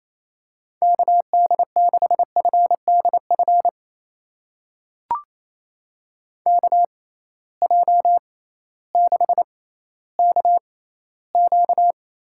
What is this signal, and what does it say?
Morse code, heard as a keyed tone, about 28 wpm, spaced as for 15 wpm, 700 Hz. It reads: KD6FDF KJ6KQ
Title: USA Call Signs - 6 Characters - 6x Character Spacing - 28wpm